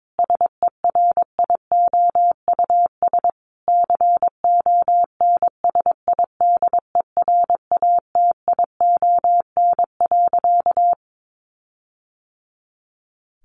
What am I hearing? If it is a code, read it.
Morse code, heard as a keyed tone, about 22 wpm, 700 Hz. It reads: SERIOUS CONSIDERATION.
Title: Sentences from Top 1500 Words 22wpm